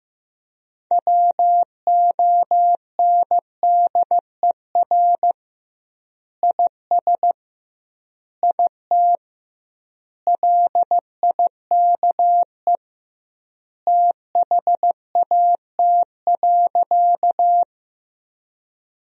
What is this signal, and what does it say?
Morse code, heard as a keyed tone, about 15 wpm, 700 Hz. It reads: WONDER IS IT LIKE THAT.
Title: 2x Word-Spacing - Sentences from Top 500 Words - 15wpm